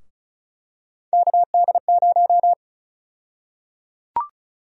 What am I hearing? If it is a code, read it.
Morse code, heard as a keyed tone, about 34 wpm, 700 Hz. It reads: KD0
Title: USA Call Sign Prefixes - 35wpm